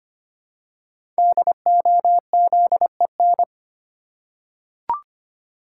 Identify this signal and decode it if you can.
Morse code, heard as a keyed tone, about 25 wpm, 700 Hz. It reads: DOZEN
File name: 5 Letter Words - First 3 Letters Given - 25wpm